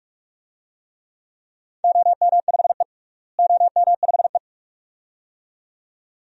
Morse code, heard as a keyed tone, about 46 wpm, 700 Hz. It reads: OM5E OM5E
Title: DXCC-100 Call Signs - 4 Characters - Repeated 2x - 45wpm